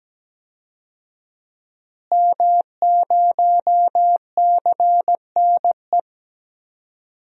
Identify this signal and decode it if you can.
Morse code, heard as a keyed tone, about 17 wpm, 700 Hz. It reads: M0CNE
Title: DXCC-100 Call Signs - 5 Characters - Repeated 3x - 17wpm